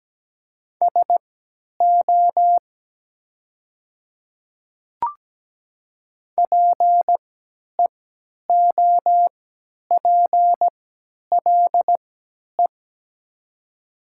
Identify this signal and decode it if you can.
Morse code, heard as a keyed tone, about 17 wpm, spaced as for 10 wpm, 700 Hz. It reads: SO PEOPLE
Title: 3x Character Spacing - Top 100 Words - 17wpm